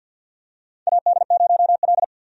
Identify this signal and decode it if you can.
Morse code, heard as a keyed tone, about 47 wpm, 700 Hz. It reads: AD0L
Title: USA Call Signs - 4 Characters - 50wpm